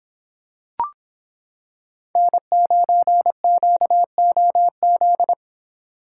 Morse code, heard as a keyed tone, about 25 wpm, 700 Hz. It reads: N9QOZ